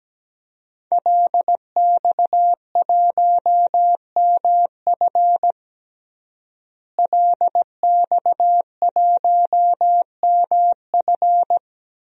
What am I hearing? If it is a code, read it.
Morse code, heard as a keyed tone, about 17 wpm, 700 Hz. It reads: LX1MF LX1MF
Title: DXCC-100 Call Signs - 5 Characters - Repeated 2x - 17wpm